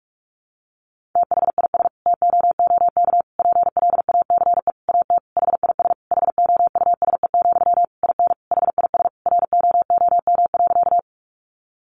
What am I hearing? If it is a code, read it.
Morse code, heard as a keyed tone, about 44 wpm, 700 Hz. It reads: THIS TOOK PLACE AT HIS HOUSE, IN HIS ROOM.
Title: Sentences from Top 300 Words 45wpm